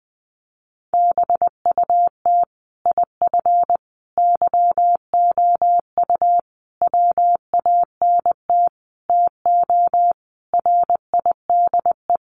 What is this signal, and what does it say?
Morse code, heard as a keyed tone, about 20 wpm, 700 Hz. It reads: BUT IF YOU WANT TO RIDE
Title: Sentences from Top 1200 Words 20wpm